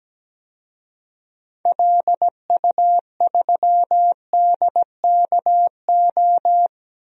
Morse code, heard as a keyed tone, about 17 wpm, 700 Hz. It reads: LU3DKO
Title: DXCC-100 Call Signs - 6 Characters - 17wpm